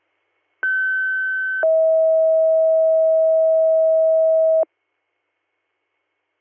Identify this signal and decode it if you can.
Two-tone page, heard as a one-second tone, then a three-second tone, about 1550 then 640 Hz. It is heard from 0.6 to 4.6 s.